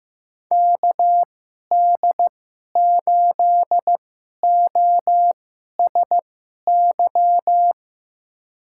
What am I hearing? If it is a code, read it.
Morse code, heard as a keyed tone, about 15 wpm, spaced as for 11 wpm, 700 Hz. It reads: KD8OSY
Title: USA Call Signs - 6 Characters - 2x Character Spacing - 15wpm